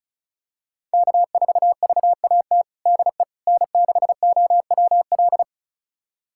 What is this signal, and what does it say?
Morse code, heard as a keyed tone, about 34 wpm, 690 Hz. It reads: K4VAT DE N6OWL
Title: USA Callsign DE Callsign - 5 Characters - 35wpm